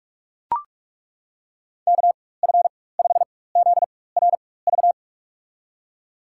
Morse code, heard as a keyed tone, about 42 wpm, spaced as for 21 wpm, 700 Hz. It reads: KF5ZRV
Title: USA Call Signs - 6 Characters - 4x Character Spacing - 45wpm